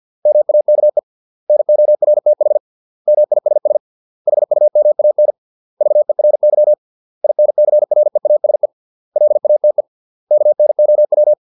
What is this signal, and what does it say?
Morse code, heard as a keyed tone, about 50 wpm, 600 Hz. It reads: MAKE NORTH MISS HUMAN VERY INCREASE LATE KNOW